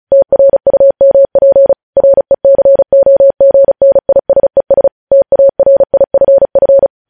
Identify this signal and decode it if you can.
Morse code, heard as a keyed tone, about 36 wpm, 550 Hz. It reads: TRUMP RECOGNISES TARIFF
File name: podcast / News headlines-1x-35x35